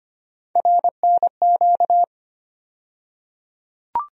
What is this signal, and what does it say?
Morse code, heard as a keyed tone, about 25 wpm, 700 Hz. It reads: RNQ
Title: USA Call Sign Postfixes - 3 Letters 25wpm